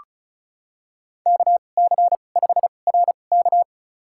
Morse code, heard as a keyed tone, about 36 wpm, spaced as for 25 wpm, 700 Hz. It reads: KC5RK
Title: USA Call Signs - 5 Characters - 2x Character Spacing - 35wpm